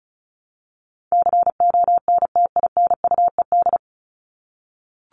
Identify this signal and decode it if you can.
Morse code, heard as a keyed tone, about 34 wpm, 700 Hz. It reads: CONTINUED
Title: ED Suffix - Top 100 Words with Base Words 35wpm